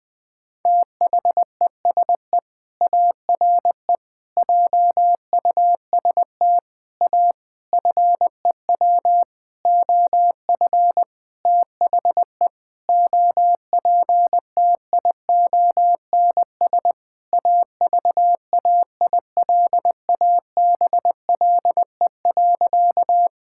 Morse code, heard as a keyed tone, about 20 wpm, 700 Hz. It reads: THESE ARE JUST A FEW OF THE OPTIONS AVAILABLE.
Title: Sentences with 7-Word Phrases - 20wpm